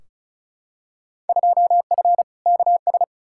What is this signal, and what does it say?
Morse code, heard as a keyed tone, about 34 wpm, 700 Hz. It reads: 2F KS
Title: Field Day Exchanges - 35wpm